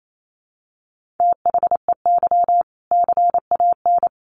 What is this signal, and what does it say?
Morse code, heard as a keyed tone, about 28 wpm, 700 Hz. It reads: THEY CAN
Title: Sentences from Top 1500 Words 28wpm